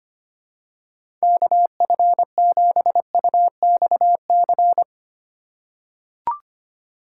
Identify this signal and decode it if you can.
Morse code, heard as a keyed tone, about 25 wpm, 700 Hz. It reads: KF7UXC
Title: K1USN SST QSO Practice - 25wpm